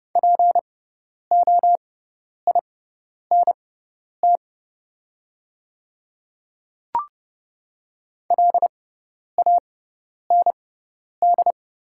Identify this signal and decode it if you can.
Morse code, heard as a keyed tone, about 30 wpm, spaced as for 10 wpm, 690 Hz. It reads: POINT LAND
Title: 6x Character Spacing - Top 200 Words - 30wpm